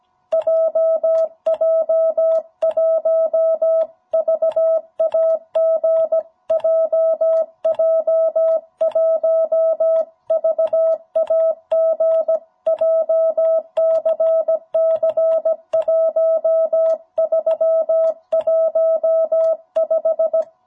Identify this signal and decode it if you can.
Morse code, heard as a keyed tone, about 17 wpm, 650 Hz. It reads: JJ1VAG JJ1VAG JCC 1315
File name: jh8cbh / 2021cw